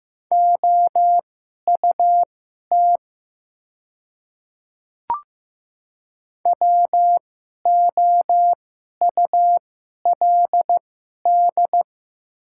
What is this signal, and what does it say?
Morse code, heard as a keyed tone, about 15 wpm, spaced as for 11 wpm, 700 Hz. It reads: OUT WOULD
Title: 2x Character Spacing - Top 100 Words - 15wpm